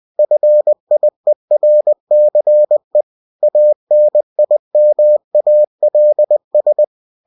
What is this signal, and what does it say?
Morse code, heard as a keyed tone, about 20 wpm, 600 Hz. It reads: FIERCE ANIMALS
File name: cw-books / winnie-the-pooh / winnie-20wpm-page-0062